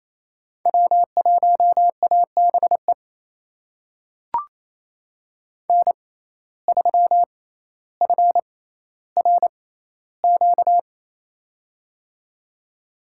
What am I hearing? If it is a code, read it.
Morse code, heard as a keyed tone, about 28 wpm, spaced as for 14 wpm, 700 Hz. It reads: W1ABE N3FRQ
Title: USA Call Signs - 6x Character Spacing - 28wpm